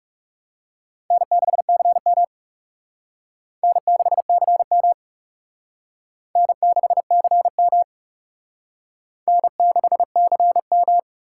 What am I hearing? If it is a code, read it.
Morse code, heard as a keyed tone, about 40 wpm, 700 Hz. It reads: N6CM N6CM N6CM N6CM